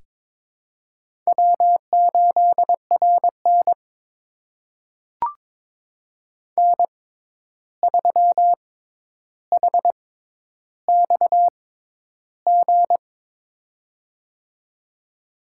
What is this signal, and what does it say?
Morse code, heard as a keyed tone, about 22 wpm, spaced as for 11 wpm, 700 Hz. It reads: W8RN N3HXG
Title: USA Call Signs - 6x Character Spacing - 22wpm